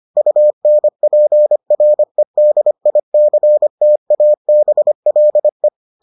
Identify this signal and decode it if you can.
Morse code, heard as a keyed tone, about 25 wpm, 600 Hz. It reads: UNPREDICTABLE